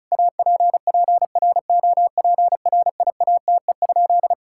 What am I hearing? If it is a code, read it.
Morse code, heard as a keyed tone, about 34 wpm, 690 Hz. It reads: APPROPRIATE?
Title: Sentences from Top 1000 Words - Review Part 5 - 35wpm